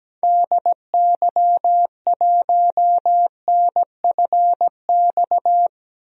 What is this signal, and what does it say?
Morse code, heard as a keyed tone, about 17 wpm, 700 Hz. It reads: DY1NFX
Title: DXCC-100 Call Signs - 6 Characters - 17wpm